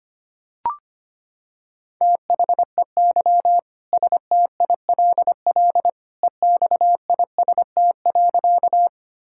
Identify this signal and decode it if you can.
Morse code, heard as a keyed tone, about 25 wpm, 700 Hz. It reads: THEY STILL EXIST.
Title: Sentences from Top 1000 Words - Review Part 5 - 25wpm